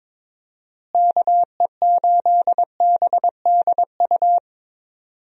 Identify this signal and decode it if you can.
Morse code, heard as a keyed tone, about 21 wpm, 700 Hz. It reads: KE8BDU
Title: USA Call Signs - 6 Characters - Repeated 3x - 22wpm